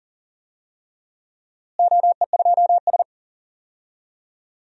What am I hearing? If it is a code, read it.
Morse code, heard as a keyed tone, about 37 wpm, 700 Hz. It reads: OE2S
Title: DXCC-100 Call Signs - 4 Characters - Repeated 3x - 40wpm